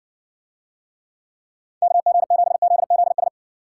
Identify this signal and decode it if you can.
Morse code, heard as a keyed tone, about 56 wpm, 700 Hz. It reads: KG7ZZS